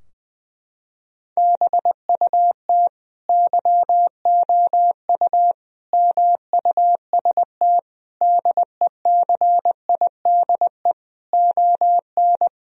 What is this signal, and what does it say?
Morse code, heard as a keyed tone, about 21 wpm, 700 Hz. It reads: BUT YOU MUST DECIDE ON
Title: Sentences from Top 1000 Words - Review Part 3 - 20wpm